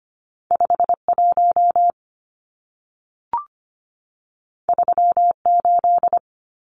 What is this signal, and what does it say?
Morse code, heard as a keyed tone, about 25 wpm, 700 Hz. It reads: 51 38
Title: Numbers - 2-Digit 25wpm